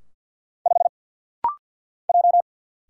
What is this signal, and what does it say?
Morse code, heard as a keyed tone, about 50 wpm, 700 Hz. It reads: 5 J
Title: Single Letter-Number Alt - Rapid-Fire 50wpm